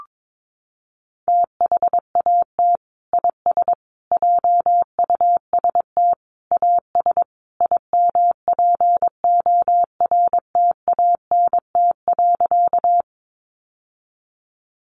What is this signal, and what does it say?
Morse code, heard as a keyed tone, about 22 wpm, 700 Hz. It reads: THAT IS JUST AS IMPORTANT.
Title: Sentences from Top 600 Words 22wpm